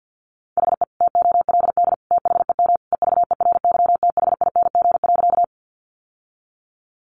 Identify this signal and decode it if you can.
Morse code, heard as a keyed tone, about 50 wpm, 700 Hz. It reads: HE TOLD THEM EVERYTHING.
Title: Sentences from Top 700 Words 50wpm